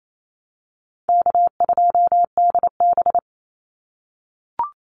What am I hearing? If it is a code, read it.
Morse code, heard as a keyed tone, about 28 wpm, 700 Hz. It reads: K2DB
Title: Easy Call Signs - 28wpm